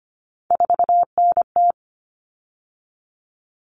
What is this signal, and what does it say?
Morse code, heard as a keyed tone, about 25 wpm, 700 Hz. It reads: 4NT
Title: Common Cut Numbers - 3-Digit 25wpm